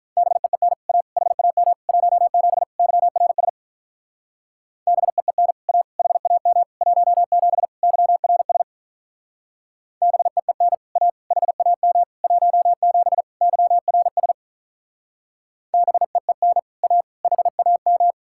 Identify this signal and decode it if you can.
Morse code, heard as a keyed tone, about 45 wpm, 700 Hz. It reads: BEEN A HAM 17 YRS BEEN A HAM 17 YRS BEEN A HAM 17 YRS BEEN A HAM